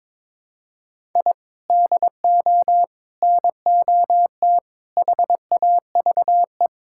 Morse code, heard as a keyed tone, about 22 wpm, 700 Hz. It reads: I DO NOT HAVE
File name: practice / 22wpm / Build A Phrase - 5-Word Phrases - 22wpm